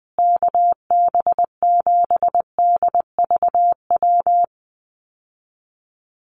Morse code, heard as a keyed tone, about 20 wpm, 700 Hz. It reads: KB7DVW
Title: USA Call Signs Repeated 3x 20wpm